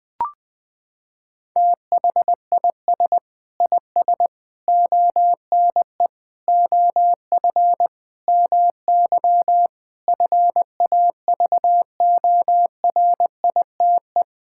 Morse code, heard as a keyed tone, about 21 wpm, 700 Hz. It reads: THIS IS ONE OF MY FAVORITE
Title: Sentences with 4-Word Phrases - 20wpm